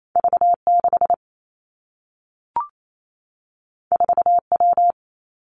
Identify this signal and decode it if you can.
Morse code, heard as a keyed tone, about 28 wpm, 700 Hz. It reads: V 6 4 W
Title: Two Challenging Letter-Number Combinations 28wpm